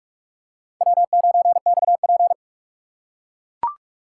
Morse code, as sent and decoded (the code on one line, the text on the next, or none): .-- ----. -..- .--.
W9XP